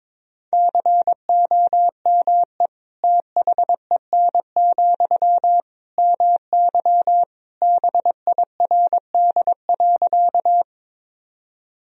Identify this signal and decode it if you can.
Morse code, heard as a keyed tone, about 22 wpm, 700 Hz. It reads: COME THEN, MY BIRD.
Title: Sentences from Top 1000 Words - Review Part 2 - 22wpm